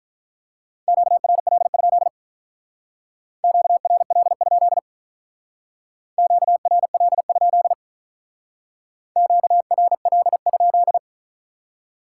Morse code, heard as a keyed tone, about 45 wpm, 700 Hz. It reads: QRL? QRL? QRL? QRL?